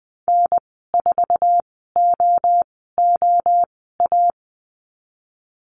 Morse code, heard as a keyed tone, about 20 wpm, spaced as for 15 wpm, 700 Hz. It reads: N4OOA
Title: 055 Lesson 10g - Letter r - Call Signs 20wpm